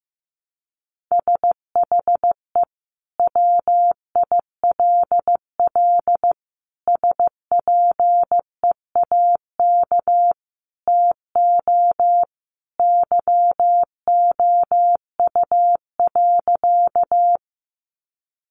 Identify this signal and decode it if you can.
Morse code, heard as a keyed tone, about 15 wpm, 700 Hz. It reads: SHE WILL SPEAK TO YOU.
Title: Sentences from Top 1100 Words 15wpm